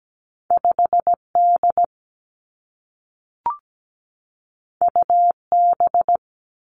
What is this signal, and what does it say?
Morse code, heard as a keyed tone, about 17 wpm, 700 Hz. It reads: 5D UB
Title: Two Challenging Letter-Number Combinations 17wpm